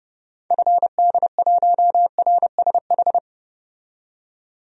Morse code, heard as a keyed tone, about 30 wpm, 700 Hz. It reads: FD1RSH
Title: DXCC-100 Call Signs - 6 Characters - 30wpm